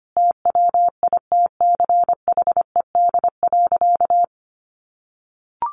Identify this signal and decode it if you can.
Morse code, heard as a keyed tone, about 25 wpm, 700 Hz. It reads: TWITCHED.
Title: A Sound Of Thunder 25wpm